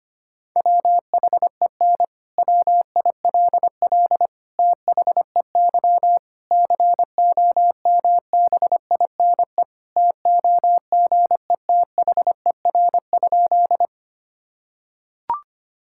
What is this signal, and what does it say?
Morse code, heard as a keyed tone, about 25 wpm, 700 Hz. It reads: WHEN WILL THEY COMBINE TOGETHER?
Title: Sentences from Top 1600 Words 25wpm